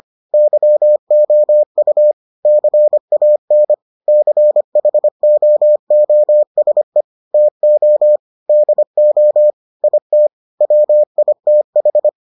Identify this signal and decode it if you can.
Morse code, heard as a keyed tone, about 25 wpm, 600 Hz. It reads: YOU CAN CHOOSE TO DO IT WITH